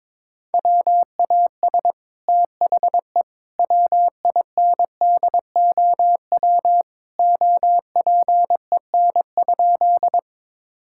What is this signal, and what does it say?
Morse code, heard as a keyed tone, about 22 wpm, 700 Hz. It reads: WAS THE WINDOW OPEN?
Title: Sentences from Top 1600 Words 22wpm